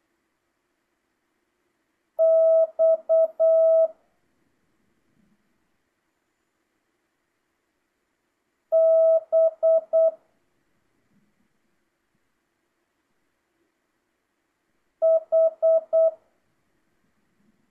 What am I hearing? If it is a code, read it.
Morse code, heard as a keyed tone, about 8 wpm, 650 Hz. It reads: X B H